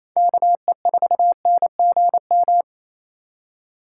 Morse code, heard as a keyed tone, about 28 wpm, 700 Hz. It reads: KE4NGM
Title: USA Call Signs - 3x Character Spacing - 28wpm